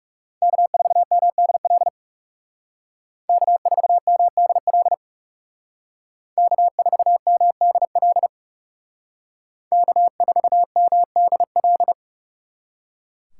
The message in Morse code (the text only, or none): K4MDL K4MDL K4MDL K4MDL